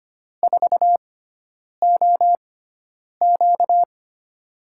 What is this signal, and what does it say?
Morse code, heard as a keyed tone, about 25 wpm, spaced as for 9 wpm, 700 Hz. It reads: 4OQ
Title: USA Call Signs - 5 Characters - 6x Character Spacing - 25wpm